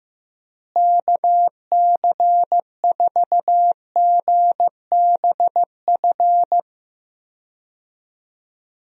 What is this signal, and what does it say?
Morse code, heard as a keyed tone, about 16 wpm, 700 Hz. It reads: KC4GBF
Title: USA Call Signs - 6 Characters - Repeated 3x - 15wpm